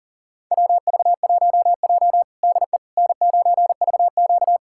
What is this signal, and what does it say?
Morse code, heard as a keyed tone, about 40 wpm, 700 Hz. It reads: WV1J DE N9VQ
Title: USA Callsign DE Callsign - 4 Characters - 40wpm